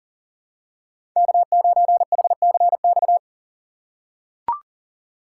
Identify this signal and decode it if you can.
Morse code, heard as a keyed tone, about 40 wpm, 700 Hz. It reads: K9HCX
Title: USA Call Signs - 5 Characters - 4x Character Spacing - 40wpm